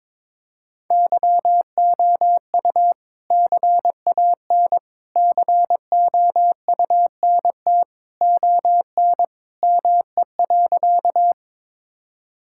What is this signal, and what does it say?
Morse code, heard as a keyed tone, about 22 wpm, 700 Hz. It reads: YOU CAN COUNT ON ME.
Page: morse-code-ninja-practice-22wpm